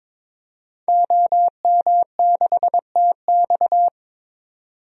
Morse code, heard as a keyed tone, about 22 wpm, 700 Hz. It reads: OM6TX
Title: DXCC-100 Call Signs - 5 Characters - 22wpm